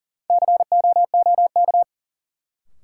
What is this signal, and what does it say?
Morse code, heard as a keyed tone, about 40 wpm, 700 Hz. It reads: COOK
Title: Sentences from Top 700 Words - New Word Drill - 40wpm